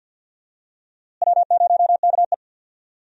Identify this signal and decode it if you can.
Morse code, heard as a keyed tone, about 47 wpm, 700 Hz. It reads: W0KE
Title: USA Call Signs - 4 Characters - Repeated 2x - 50wpm